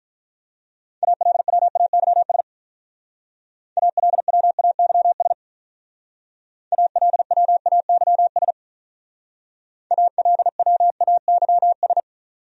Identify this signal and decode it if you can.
Morse code, heard as a keyed tone, about 46 wpm, 700 Hz. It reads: ALWAYS ALWAYS ALWAYS ALWAYS